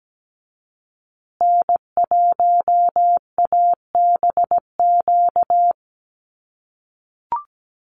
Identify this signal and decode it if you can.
Morse code, heard as a keyed tone, about 17 wpm, 700 Hz. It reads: N1ABQ
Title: USA Call Signs Repeated 3x 17wpm